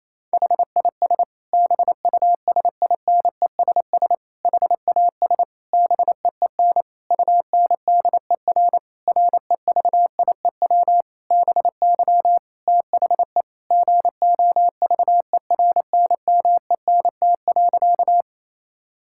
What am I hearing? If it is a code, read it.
Morse code, heard as a keyed tone, about 28 wpm, 700 Hz. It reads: HIS BUSINESS HAS BEEN UNDER REVIEW BY THE GOVERNMENT.
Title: Sentences from Top 1000 Words - Review Part 6 - 28wpm